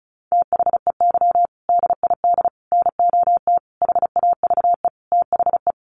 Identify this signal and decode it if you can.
Morse code, heard as a keyed tone, about 34 wpm, 700 Hz. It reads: THEY DID NOT HAVE THE
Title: Sentences from Top 1400 Words 35wpm